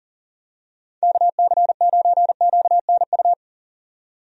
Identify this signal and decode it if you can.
Morse code, heard as a keyed tone, about 40 wpm, 700 Hz. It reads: KC9QNU
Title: USA Call Signs - 6 Characters - 40wpm